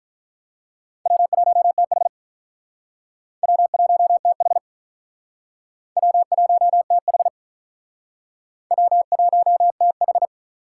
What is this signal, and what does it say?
Morse code, heard as a keyed tone, about 44 wpm, 700 Hz. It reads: W1TH W1TH W1TH W1TH